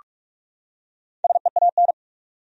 Morse code, heard as a keyed tone, about 46 wpm, 700 Hz. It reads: SEAN
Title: USA Common Names - Max 5 Letters - 45wpm